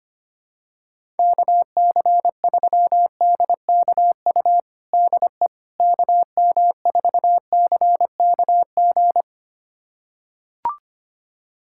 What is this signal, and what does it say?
Morse code, heard as a keyed tone, about 25 wpm, 700 Hz. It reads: KC3DKU DE KM4CKG
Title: USA Callsign DE Callsign 25wpm